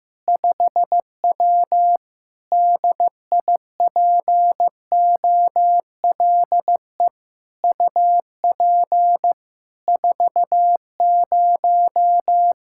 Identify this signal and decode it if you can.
Morse code, heard as a keyed tone, about 15 wpm, 700 Hz. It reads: 5W DIPOLE UP 40
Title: Practice QSO - Exchange 2 - Info - 15wpm